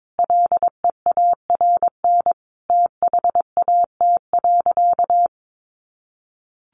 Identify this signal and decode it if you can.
Morse code, heard as a keyed tone, about 22 wpm, 700 Hz. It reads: LEARN THAT.
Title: Sentences from Top 1200 Words 22wpm